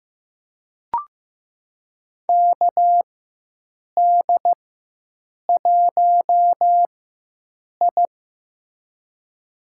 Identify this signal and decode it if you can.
Morse code, heard as a keyed tone, about 15 wpm, spaced as for 7 wpm, 700 Hz. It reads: KD1I
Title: USA Call Signs - 4 Characters - 4x Character Spacing - 15wpm